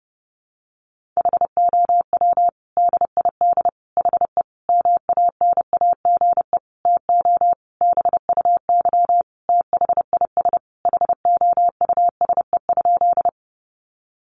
Morse code, heard as a keyed tone, about 30 wpm, 700 Hz. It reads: HOW DID HE MANAGE TO BUY THIS HOUSE?
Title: Sentences from Top 1400 Words 30wpm